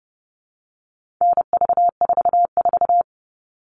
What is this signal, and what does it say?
Morse code, heard as a keyed tone, about 30 wpm, 700 Hz. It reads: NV44
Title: All Cut Numbers - 4-Digit 30wpm